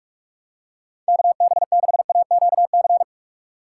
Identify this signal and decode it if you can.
Morse code, heard as a keyed tone, about 46 wpm, 700 Hz. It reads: KB6AQC
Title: USA Call Signs - 6 Characters - 3x Character Spacing - 45wpm